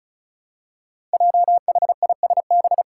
Morse code, heard as a keyed tone, about 34 wpm, 700 Hz. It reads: JHISB
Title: Five Challenging Letter-Number Combinations 35wpm